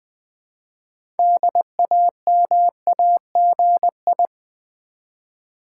Morse code, heard as a keyed tone, about 20 wpm, 700 Hz. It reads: DAMAGI
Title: Build A Word - 2 Letters At A Time - 8-Letter Words - 20wpm